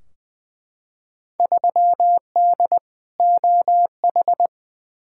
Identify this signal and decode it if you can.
Morse code, heard as a keyed tone, about 20 wpm, 700 Hz. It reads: 3D OH
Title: Field Day Exchanges - 20wpm